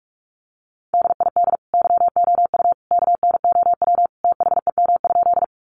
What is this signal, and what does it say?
Morse code, heard as a keyed tone, about 44 wpm, 700 Hz. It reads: DID YOU KNOW THEM?
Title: Sentences from Top 100 Words 45wpm